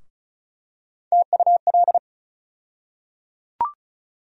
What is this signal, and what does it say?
Morse code, heard as a keyed tone, about 36 wpm, 700 Hz. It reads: TUL
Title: USA Call Sign Postfixes - 3 Letters 35wpm